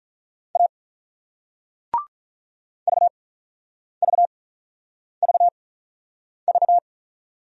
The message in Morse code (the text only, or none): A VVVV